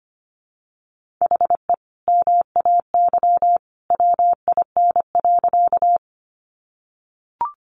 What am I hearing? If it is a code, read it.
Morse code, heard as a keyed tone, about 25 wpm, 700 Hz. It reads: HE MAY WIN.
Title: Sentences from Top 1500 Words 25wpm